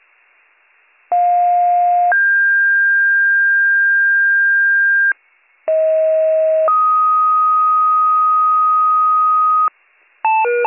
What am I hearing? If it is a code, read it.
Two-tone page, heard as a one-second tone, then a three-second tone, about 710 then 1600 Hz, about 640 then 1250 Hz, with noise. It is heard from 1.1 to 9.7 s.